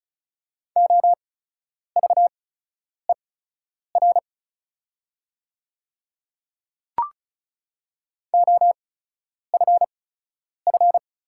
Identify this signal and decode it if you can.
Morse code, heard as a keyed tone, about 36 wpm, spaced as for 10 wpm, 700 Hz. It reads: OVER OFF
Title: 8x Character Spacing - Top 200 Words - 35wpm